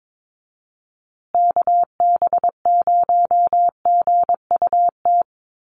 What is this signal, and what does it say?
Morse code, heard as a keyed tone, about 22 wpm, 700 Hz. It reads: KB0GUT
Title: USA Call Signs Repeated 2x 22wpm